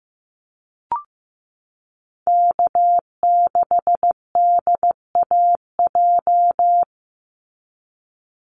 Morse code, heard as a keyed tone, about 15 wpm, 700 Hz. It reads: K6DAJ
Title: Call Signs Repeated 2x - 15wpm